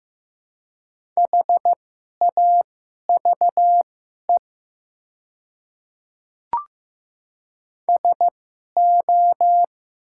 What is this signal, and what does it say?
Morse code, heard as a keyed tone, about 15 wpm, spaced as for 11 wpm, 700 Hz. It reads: HAVE SO
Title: 2x Character Spacing - Top 100 Words - 15wpm